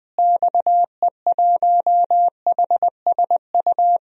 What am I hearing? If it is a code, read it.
Morse code, heard as a keyed tone, about 21 wpm, 700 Hz. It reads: XE1HSU
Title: DXCC-100 Call Signs - 6 Characters - Repeated 3x - 20wpm